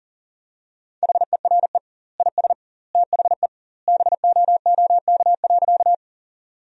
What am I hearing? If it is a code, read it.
Morse code, heard as a keyed tone, about 40 wpm, 700 Hz. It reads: HERE IS THE BOOK.
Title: 2x Word-Spacing - Sentences from Top 300 Words - 40wpm